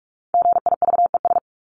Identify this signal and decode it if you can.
Morse code, heard as a keyed tone, about 44 wpm, 700 Hz. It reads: GIVES
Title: ES:S Suffix - Top 100 Words with Base Words 45wpm